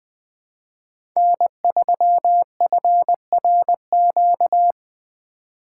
Morse code, heard as a keyed tone, about 21 wpm, 700 Hz. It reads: N3FRQ
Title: USA Call Signs - 5 Characters - 6x Character Spacing - 20wpm